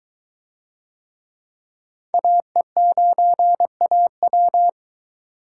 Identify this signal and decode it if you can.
Morse code, heard as a keyed tone, about 21 wpm, 700 Hz. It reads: AE9AW